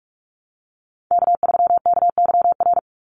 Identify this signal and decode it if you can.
Morse code, heard as a keyed tone, about 44 wpm, 700 Hz. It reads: K3KYR
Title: NA SOTA Chaser Callsigns 45wpm